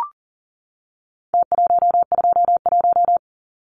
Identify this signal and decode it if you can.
Morse code, heard as a keyed tone, about 40 wpm, 700 Hz. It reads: T121
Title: Numbers - Contest Style - 4-Digit 40wpm